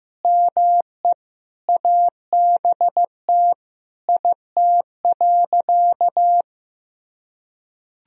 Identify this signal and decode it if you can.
Morse code, heard as a keyed tone, about 15 wpm, 700 Hz. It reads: ME ABT IT.
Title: Sentences from Top 200 Words and Abbreviations 15wpm